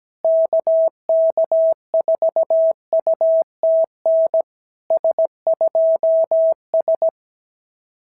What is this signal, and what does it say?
Morse code, heard as a keyed tone, about 17 wpm, 650 Hz. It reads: KK4UTN S2S
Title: SOTA QSO Practice 17wpm